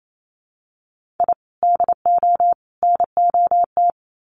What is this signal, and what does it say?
Morse code, heard as a keyed tone, about 28 wpm, 700 Hz. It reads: I DO NOT
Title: Sets of 3 Words 28wpm